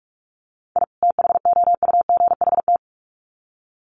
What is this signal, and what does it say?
Morse code, heard as a keyed tone, about 44 wpm, 700 Hz. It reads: I THOUGHT
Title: Sets of 2 Words 45wpm